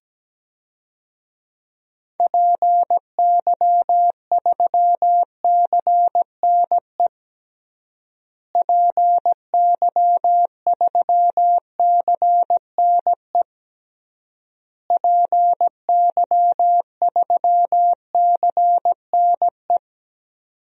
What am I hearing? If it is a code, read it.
Morse code, heard as a keyed tone, about 17 wpm, 700 Hz. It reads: PY3CNE PY3CNE PY3CNE
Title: DXCC-100 Call Signs - 6 Characters - Repeated 3x - 17wpm